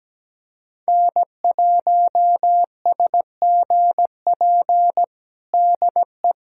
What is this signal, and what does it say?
Morse code, heard as a keyed tone, about 17 wpm, 700 Hz. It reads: N1SGP DE
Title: USA Callsign DE Callsign - 5 Characters - 17wpm